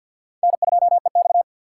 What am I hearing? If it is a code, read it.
Morse code, heard as a keyed tone, about 48 wpm, 700 Hz. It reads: N2EX
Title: USA Call Signs - 4 Characters - 2x Character Spacing - 50wpm